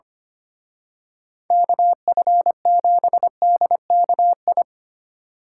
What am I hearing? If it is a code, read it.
Morse code, heard as a keyed tone, about 28 wpm, 700 Hz. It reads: KF7DKI
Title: Speed Racing - USA 2x3 Call Signs 25wpm